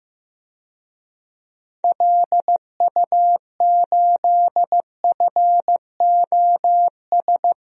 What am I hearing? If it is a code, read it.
Morse code, heard as a keyed tone, about 15 wpm, 700 Hz. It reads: LU8FOS
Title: DXCC-100 Call Signs - 6 Characters - Repeated 3x - 15wpm